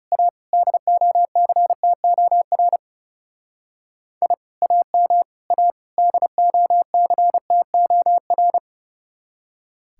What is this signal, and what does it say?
Morse code, heard as a keyed tone, about 36 wpm, 700 Hz. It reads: A DOCTOR I AM A DOCTOR
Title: Speed-Racing Common QSO Phrases - 30wpm